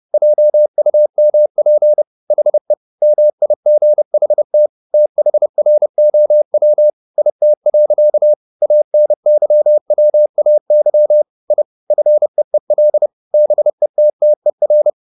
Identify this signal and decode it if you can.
Morse code, heard as a keyed tone, about 30 wpm, 600 Hz. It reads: JUMP HE MIGHT THROW IT. ANYWAY I FEEL BETTER
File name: the old man and the sea / the old man and the sea 1 30wpm